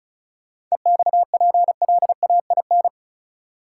Morse code, heard as a keyed tone, about 34 wpm, 700 Hz. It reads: EXPLAIN
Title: Sentences from Top 1100 Words - New Word Drill Alt - 35wpm